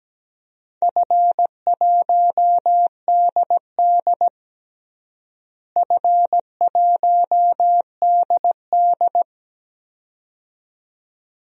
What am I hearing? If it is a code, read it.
Morse code, heard as a keyed tone, about 17 wpm, 700 Hz. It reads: F1DD F1DD
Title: DXCC-100 Call Signs - 4 Characters - Repeated 2x - 17wpm